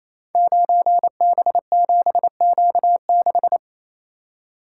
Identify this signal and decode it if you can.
Morse code, heard as a keyed tone, about 28 wpm, 700 Hz. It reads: 9B7Q6
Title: Code Groups - 5 Letter:Numbers - 28wpm